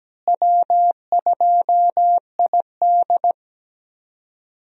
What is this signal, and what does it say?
Morse code, heard as a keyed tone, about 17 wpm, 700 Hz. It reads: W2ID
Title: USA Call Signs - 4 Characters - Repeated 3x - 17wpm